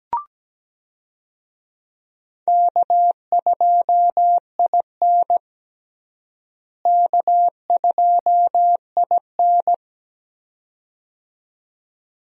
Morse code, heard as a keyed tone, about 17 wpm, 700 Hz. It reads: K2IN K2IN
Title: USA Call Signs - 4 Characters - Repeated 2x - 17wpm